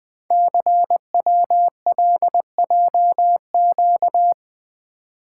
Morse code, heard as a keyed tone, about 20 wpm, 700 Hz. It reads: CWLJQ
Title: Code Groups - 5 Letters - 20wpm